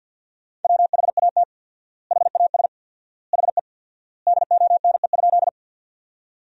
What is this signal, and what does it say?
Morse code, heard as a keyed tone, about 50 wpm, 700 Hz. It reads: WHAT HAS HE DONE?
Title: 4x Word-Spacing - Sentences from Top 500 Words - 50wpm